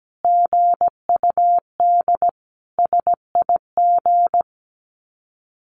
Morse code, heard as a keyed tone, about 17 wpm, 700 Hz. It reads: GUD SIG
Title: Common QSO Phrases 17wpm